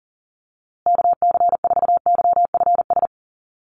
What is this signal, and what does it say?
Morse code, heard as a keyed tone, about 40 wpm, 700 Hz. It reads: KC4YFS
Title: USA Call Signs Repeated 3x 40wpm